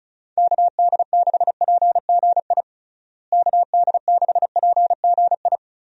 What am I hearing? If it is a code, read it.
Morse code, heard as a keyed tone, about 34 wpm, 700 Hz. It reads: KD6PGI KD6PGI
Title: USA Call Signs - 6 Characters - Repeated 3x - 35wpm